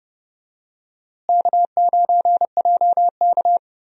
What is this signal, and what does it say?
Morse code, heard as a keyed tone, about 30 wpm, 700 Hz. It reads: K9JK
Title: USA Call Signs - 4 Characters - Repeated 2x - 30wpm